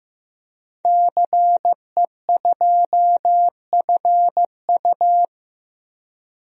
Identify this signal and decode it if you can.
Morse code, heard as a keyed tone, about 15 wpm, 700 Hz. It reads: CE2FU
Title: DXCC-100 Call Signs - 5 Characters - 15wpm